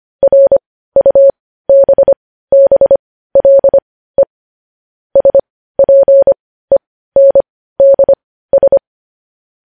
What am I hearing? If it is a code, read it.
Morse code, heard as a keyed tone, about 25 wpm, spaced as for 15 wpm, 550 Hz. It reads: RUBBLE SPENDS
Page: news-headlines-in-morse-code-at-15-wpm